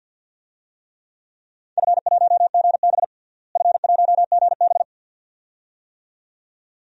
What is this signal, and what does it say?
Morse code, heard as a keyed tone, about 47 wpm, 700 Hz. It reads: F1GB F1GB
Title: DXCC-100 Call Signs - 4 Characters - Repeated 2x - 50wpm